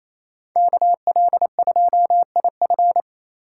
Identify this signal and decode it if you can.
Morse code, heard as a keyed tone, about 28 wpm, 700 Hz. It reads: KL2IF
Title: USA Call Signs - 5 Characters - 2x Character Spacing - 28wpm